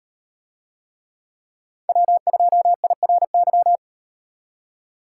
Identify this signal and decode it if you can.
Morse code, heard as a keyed tone, about 36 wpm, 700 Hz. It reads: W2IRY